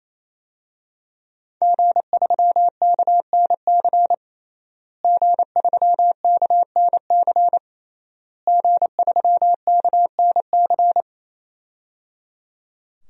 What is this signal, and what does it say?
Morse code, heard as a keyed tone, about 28 wpm, 700 Hz. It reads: G3KNC G3KNC G3KNC
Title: DXCC-100 Call Signs - 5 Characters - Repeated 3x - 28wpm